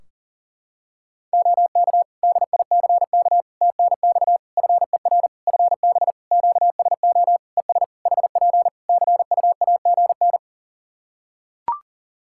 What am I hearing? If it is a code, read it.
Morse code, heard as a keyed tone, about 40 wpm, 700 Hz. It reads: OK DICK TNX FER FB QSO ES HP CUAGN
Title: Practice QSO - Exchange 4 - Ending - 40wpm